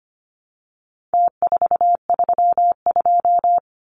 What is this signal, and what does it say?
Morse code, heard as a keyed tone, about 25 wpm, 700 Hz. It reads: T432
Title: Numbers - Contest Style - 4-Digit 25wpm